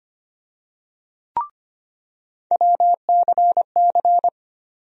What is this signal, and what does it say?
Morse code, heard as a keyed tone, about 25 wpm, 700 Hz. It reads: WCC
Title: USA Call Sign Postfixes - 3 Letters 25wpm